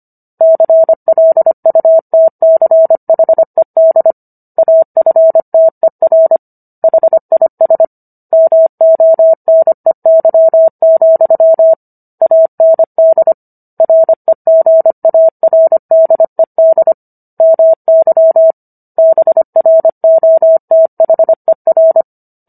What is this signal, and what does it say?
Morse code, heard as a keyed tone, about 25 wpm, 650 Hz. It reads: CLUTCHED AFTER HIS MONEY, AND REGARDED MY BROTHER